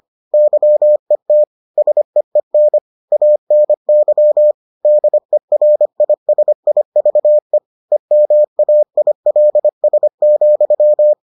Morse code, heard as a keyed tone, about 25 wpm, 600 Hz. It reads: YET SEEN ANY DERISIVE EMAILS,